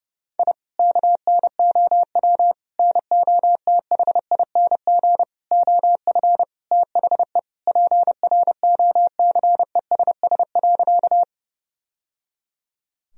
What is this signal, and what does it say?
Morse code, heard as a keyed tone, about 30 wpm, 700 Hz. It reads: I KNOW NOTHING OF THE PROCESS.
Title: Sentences from Top 1000 Words - Review Part 3 - 30wpm